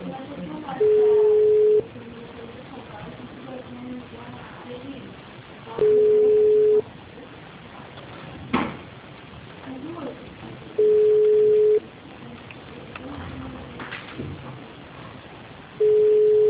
Ringback tone (Europe, 425 Hz).